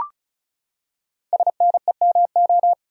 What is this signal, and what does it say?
Morse code, heard as a keyed tone, about 36 wpm, 700 Hz. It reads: SNEMO
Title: Code Groups - 5 Letters - 35wpm